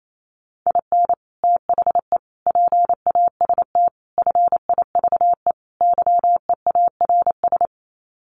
Morse code, heard as a keyed tone, about 28 wpm, 700 Hz. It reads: IN THE PAST FIVE YEARS
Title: Sets of 5 Words Encore 28wpm